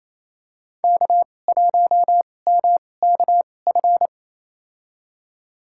Morse code, heard as a keyed tone, about 28 wpm, spaced as for 20 wpm, 700 Hz. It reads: K1MKF
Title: USA Call Signs - 5 Characters - 2x Character Spacing - 28wpm